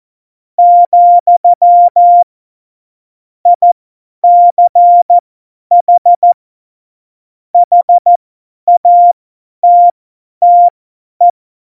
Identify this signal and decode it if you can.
Morse code, heard as a keyed tone, about 14 wpm, spaced as for 10 wpm, 700 Hz. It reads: , ICH HATTE
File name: Morsekurs / Witze / Witz 29. September 2020, 50 F70